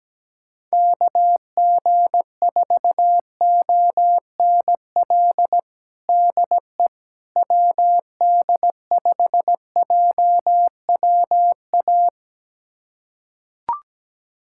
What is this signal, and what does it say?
Morse code, heard as a keyed tone, about 17 wpm, 700 Hz. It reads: KG4ONL DE WD5JWA
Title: USA Callsign DE Callsign - 6 Characters - 17wpm